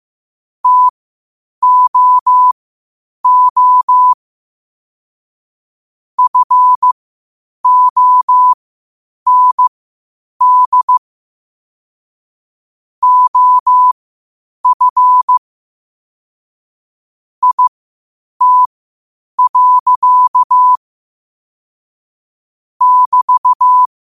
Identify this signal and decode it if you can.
Morse code, heard as a keyed tone, about 15 wpm, spaced as for 8 wpm, 1000 Hz. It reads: TOO FOND OF IT. <BT>
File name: qotdfiles / 8wpm